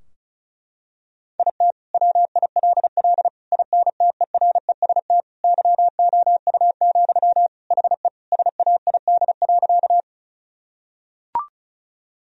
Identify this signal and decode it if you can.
Morse code, heard as a keyed tone, about 36 wpm, 700 Hz. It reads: IT WILL INTEREST YOU, HE SAID.
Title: Sentences from Top 1000 Words - Review Part 2 - 35wpm